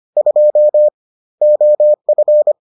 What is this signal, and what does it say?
Morse code, heard as a keyed tone, about 25 wpm, 600 Hz. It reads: 2 OF